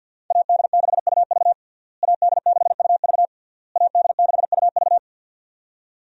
Morse code, heard as a keyed tone, about 51 wpm, 700 Hz. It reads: AD6UV AD6UV AD6UV
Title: USA Call Signs - 5 Characters - Repeated 3x - 50wpm